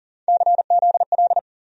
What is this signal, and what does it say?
Morse code, heard as a keyed tone, about 40 wpm, 700 Hz. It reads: CZL
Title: USA Call Sign Postfixes - 3 Letters 40wpm